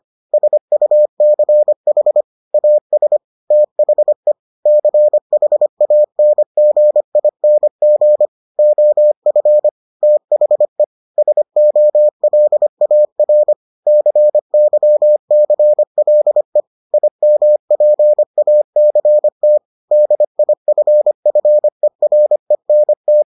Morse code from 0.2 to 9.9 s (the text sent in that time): SUCH AS THE CHANGING OF